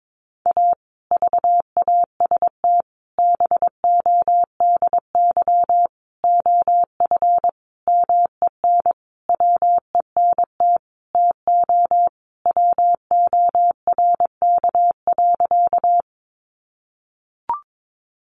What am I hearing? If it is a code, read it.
Morse code, heard as a keyed tone, about 22 wpm, 700 Hz. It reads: A VAST BODY OF MEN WENT TO WORK.
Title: Sentences from Top 1300 Words 22wpm